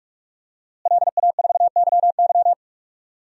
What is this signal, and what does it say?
Morse code, heard as a keyed tone, about 44 wpm, 700 Hz. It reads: LA4YY
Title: DXCC-100 Call Signs - 5 Characters - Repeated 2x - 45wpm